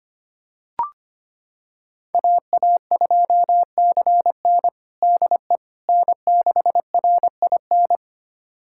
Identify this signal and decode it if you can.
Morse code, heard as a keyed tone, about 25 wpm, 700 Hz. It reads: AA2CN DE N6RIN
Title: USA Callsign DE Callsign - 5 Characters - 25wpm